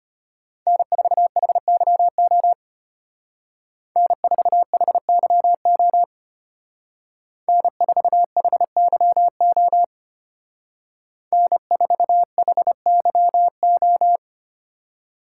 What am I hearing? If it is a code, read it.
Morse code, heard as a keyed tone, about 31 wpm, 700 Hz. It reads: N4HYO N4HYO N4HYO N4HYO
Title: Speed Racing - USA 1x3 Call Signs 25wpm